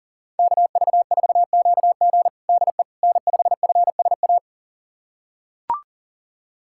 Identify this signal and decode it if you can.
Morse code, heard as a keyed tone, about 40 wpm, 700 Hz. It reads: KV4QG DE N5FSA
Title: USA Callsign DE Callsign - 5 Characters - 40wpm